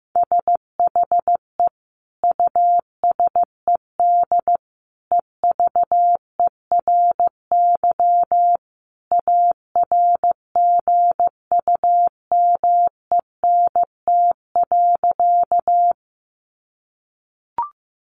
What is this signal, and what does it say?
Morse code, heard as a keyed tone, about 15 wpm, 700 Hz. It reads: SHE USED EVERY ARGUMENT.
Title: Sentences from Top 1400 Words 15wpm